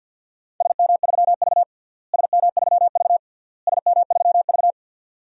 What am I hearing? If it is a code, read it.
Morse code, heard as a keyed tone, about 51 wpm, 700 Hz. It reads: SM3V SM3V SM3V
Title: DXCC-100 Call Signs - 4 Characters - Repeated 3x - 50wpm